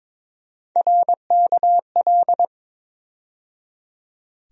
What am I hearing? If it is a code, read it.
Morse code, heard as a keyed tone, about 22 wpm, 700 Hz. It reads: RKL
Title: USA Call Sign Postfixes - 3 Letters 22wpm